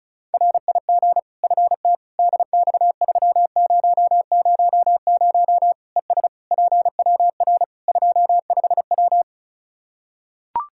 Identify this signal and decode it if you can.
Morse code, heard as a keyed tone, about 36 wpm, 700 Hz. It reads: RIG FT DX3000 ES PWR 25W
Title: K1USN SST QSO Practice 35wpm